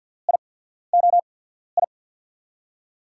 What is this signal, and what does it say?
Morse code, heard as a keyed tone, about 50 wpm, spaced as for 14 wpm, 700 Hz. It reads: IOI
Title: USA Call Signs - 6 Characters - 8x Character Spacing - 50wpm